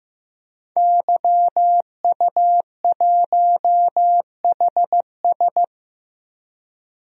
Morse code, heard as a keyed tone, about 15 wpm, 700 Hz. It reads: YU1HS
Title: DXCC-100 Call Signs - 5 Characters - Repeated 2x - 15wpm